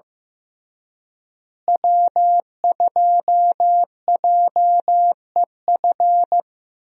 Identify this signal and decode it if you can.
Morse code, heard as a keyed tone, about 16 wpm, 700 Hz. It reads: W2JEF
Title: USA Call Signs - 5 Characters - Repeated 2x - 15wpm